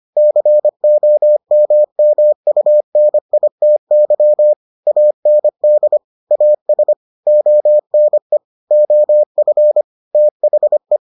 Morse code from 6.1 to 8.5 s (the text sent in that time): AS ONE